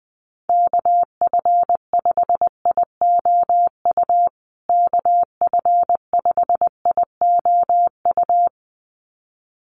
Morse code, heard as a keyed tone, about 20 wpm, 700 Hz. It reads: KF5IOU KF5IOU
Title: USA Call Signs Repeated 2x 20wpm